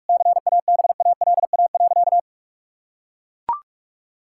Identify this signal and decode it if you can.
Morse code, heard as a keyed tone, about 41 wpm, 700 Hz. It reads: CABALA.
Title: Pangram Challenge v2 - 45wpm